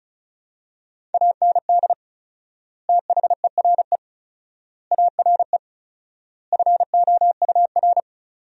Morse code, heard as a keyed tone, about 36 wpm, 700 Hz. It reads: AND THERE ARE FOUR
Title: 4x Word-Spacing - Sentences from Top 300 Words - 35wpm